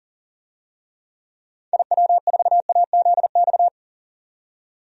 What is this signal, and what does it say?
Morse code, heard as a keyed tone, about 37 wpm, 700 Hz. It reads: IW4AZX
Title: DXCC-100 Call Signs - 6 Characters - 40wpm